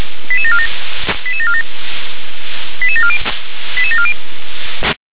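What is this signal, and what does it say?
Five-tone selective call (ZVEI), heard as quick runs of five tones, tones of about 70 ms. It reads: 88747 88747 88740 88740
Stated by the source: Five-tone ZVEI selective call received on SDR.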